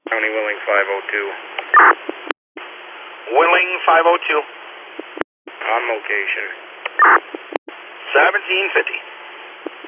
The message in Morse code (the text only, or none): none